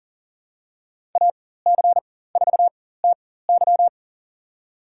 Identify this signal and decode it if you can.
Morse code, heard as a keyed tone, about 40 wpm, spaced as for 19 wpm, 700 Hz. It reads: AC4TY
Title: USA Call Signs - 5 Characters - 4x Character Spacing - 40wpm